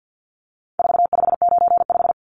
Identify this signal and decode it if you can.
Morse code, heard as a keyed tone, about 48 wpm, 700 Hz. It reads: 4595
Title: Numbers - 4-Digit 50wpm